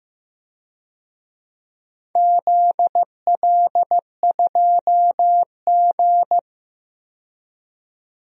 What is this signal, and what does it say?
Morse code, heard as a keyed tone, about 15 wpm, 700 Hz. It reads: ZL2G
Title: DXCC-100 Call Signs - 4 Characters - 15wpm